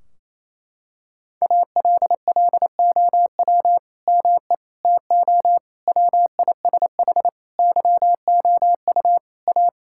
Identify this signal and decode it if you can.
Morse code, heard as a keyed tone, about 28 wpm, 700 Hz. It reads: ALLOW ME TO WISH YOU A
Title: Sentences from Top 1600 Words 28wpm